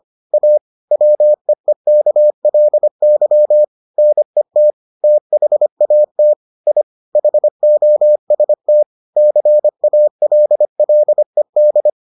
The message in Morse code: A WEEKLY NET THAT I HOST CALLED

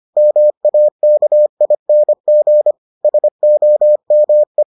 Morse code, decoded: MAKING SOME

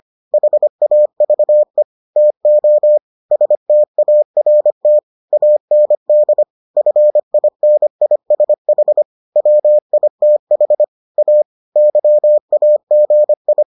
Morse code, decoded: HAVE TO START AND FINISH WITH A YAGI

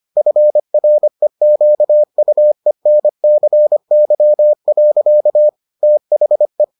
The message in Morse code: FREQUENCY. THE

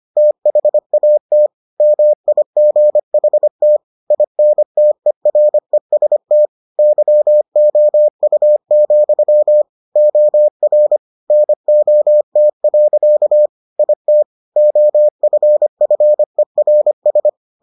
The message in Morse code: THAT MIGHT INTEREST YOU, OR NOT. IT OFFERS

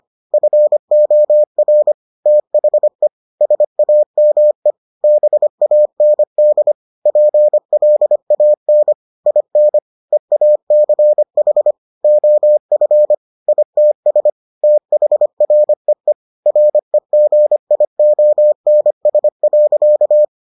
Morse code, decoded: FOR THE SAME BAND PLAN IN EACH OF ITS THREE REGIONS.

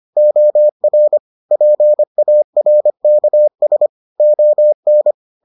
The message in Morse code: OR PARKS ON

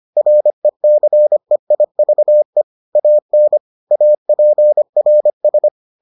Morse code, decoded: RECEIVE AN APRS

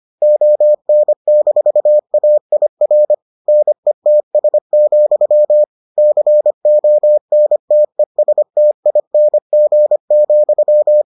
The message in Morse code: ON-AIR NETS, CONTESTING,

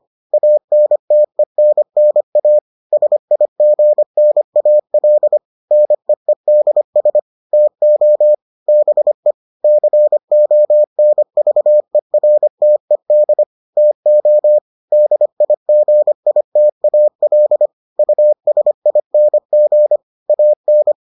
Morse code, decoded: ANTENNA SIGNAL NEEDS TO BE CONVERTED TO DIGITAL USING AN

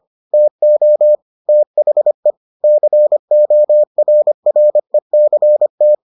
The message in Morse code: TO THE CORRECT

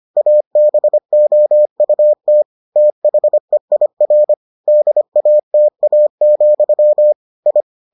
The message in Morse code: ABOUT THEIR DATA, I

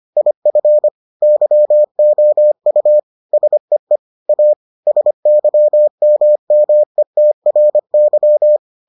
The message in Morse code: IF YOU SEE A SYMMETRY

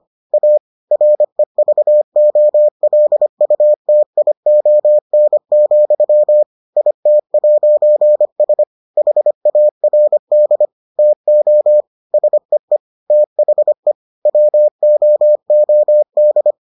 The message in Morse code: A REVOLUTION, IT'S HARD TO SEE THE WOOD

